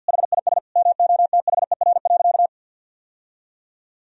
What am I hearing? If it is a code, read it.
Morse code, heard as a keyed tone, about 50 wpm, 700 Hz. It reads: HIS MOTHER.
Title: Sentences from Top 1000 Words - Review Part 3 - 50wpm